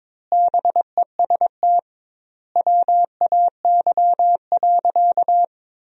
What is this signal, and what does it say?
Morse code, heard as a keyed tone, about 22 wpm, 700 Hz. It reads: BEST WAY.
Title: 2x Word-Spacing - Sentences from Top 400 Words - 22wpm